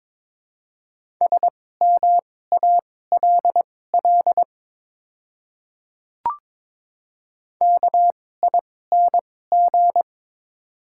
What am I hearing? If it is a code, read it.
Morse code, heard as a keyed tone, about 23 wpm, spaced as for 16 wpm, 700 Hz. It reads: SMALL KING
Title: 2x Character Spacing - Top 500 Words - 22wpm